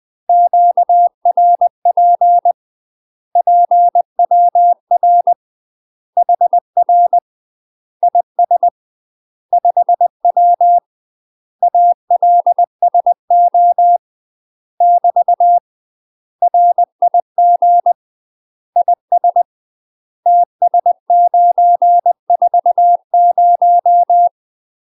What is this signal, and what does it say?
Morse code, heard as a keyed tone, about 20 wpm, 700 Hz. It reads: QRP PWR HR IS 5W ALSO <BT> RIG IS TS940
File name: UK MASTER QSOs TRAINING/20WPM